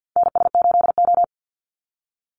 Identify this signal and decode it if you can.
Morse code, heard as a keyed tone, about 50 wpm, 700 Hz. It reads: NS8O
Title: Call Signs - 50wpm